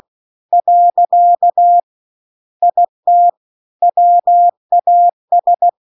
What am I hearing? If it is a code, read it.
Morse code, heard as a keyed tone, about 16 wpm, 700 Hz. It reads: . IT WAS